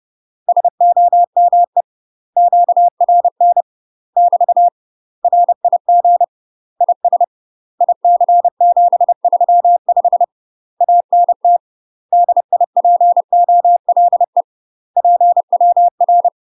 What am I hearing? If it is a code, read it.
Morse code, heard as a keyed tone, about 30 wpm, 700 Hz. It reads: SOME QRN <BT> RIG IS IC735 ANT DIPOLE PWR
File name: UK MASTER QSOs TRAINING / 30WPM / UK MASTER QSOs 0002